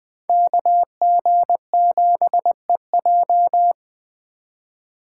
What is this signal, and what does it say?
Morse code, heard as a keyed tone, about 23 wpm, 700 Hz. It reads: KG7EJ
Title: Speed Racing - USA 2x2 Call Signs 20wpm